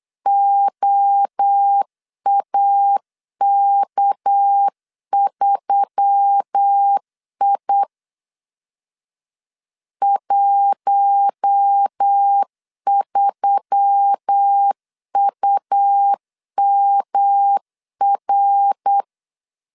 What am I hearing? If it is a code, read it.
Morse code, heard as a keyed tone, about 8 wpm, 800 Hz. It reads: OAK3I 13UMR
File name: CW / DET / 1 6 1500C S B 8wpm 5142